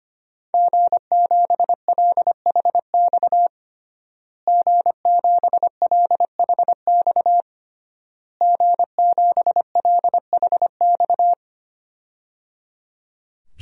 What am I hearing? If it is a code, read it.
Morse code, heard as a keyed tone, about 25 wpm, 700 Hz. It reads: G7LHX G7LHX G7LHX
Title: DXCC-100 Call Signs - 5 Characters - Repeated 3x - 25wpm